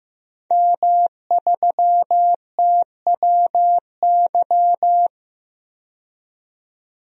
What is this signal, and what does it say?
Morse code, heard as a keyed tone, about 15 wpm, 700 Hz. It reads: M3TWY
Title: DXCC-100 Call Signs - 5 Characters - 15wpm